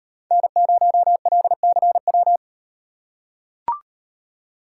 Morse code, heard as a keyed tone, about 38 wpm, 700 Hz. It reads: N0LCW